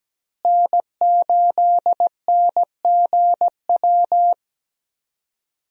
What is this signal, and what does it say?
Morse code, heard as a keyed tone, about 17 wpm, 700 Hz. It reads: N8NGW
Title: USA Call Signs - 5 Characters - Repeated 2x - 17wpm